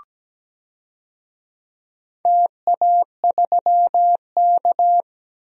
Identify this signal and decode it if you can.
Morse code, heard as a keyed tone, about 17 wpm, 700 Hz. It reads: TA3K
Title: DXCC-100 Call Signs - 4 Characters - Repeated 3x - 17wpm